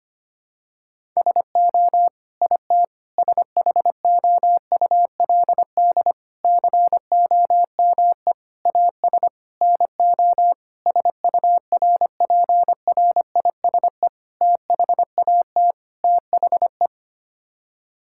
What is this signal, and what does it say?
Morse code, heard as a keyed tone, about 25 wpm, 700 Hz. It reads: SO IT SHOULD COME AS NO SURPRISE THAT THE
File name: Sets of 9 Words - 25wpm